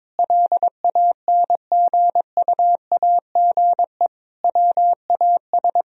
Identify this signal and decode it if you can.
Morse code, heard as a keyed tone, about 22 wpm, 700 Hz. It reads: LANGUAGE WAS
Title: Sentences from Top 1000 Words - Review Part 4 - 22wpm